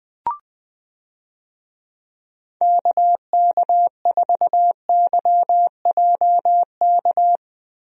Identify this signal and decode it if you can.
Morse code, heard as a keyed tone, about 20 wpm, 700 Hz. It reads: KK4YJK
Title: USA Call Signs - 6 Characters - Repeated 2x - 20wpm